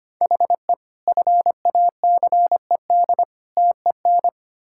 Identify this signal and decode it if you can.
Morse code, heard as a keyed tone, about 25 wpm, 700 Hz. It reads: HE FACED TEN
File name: practice / 25wpm / Sentences with 9-Word Phrases - 25wpm